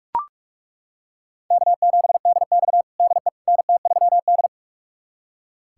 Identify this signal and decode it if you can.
Morse code, heard as a keyed tone, about 41 wpm, 700 Hz. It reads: K7DX DE NT3D
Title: USA Callsign DE Callsign - 4 Characters - 45wpm